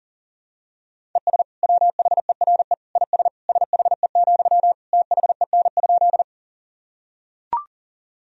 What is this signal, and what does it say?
Morse code, heard as a keyed tone, about 40 wpm, 700 Hz. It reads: ES WHERE IS SHE, THEN?
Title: Sentences from Top 200 Words and Abbreviations 40wpm